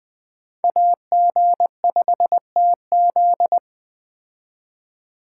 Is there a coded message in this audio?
Yes: Morse code from 0.6 to 3.6 s, heard as a keyed tone, about 19 wpm, 700 Hz. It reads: AG5TZ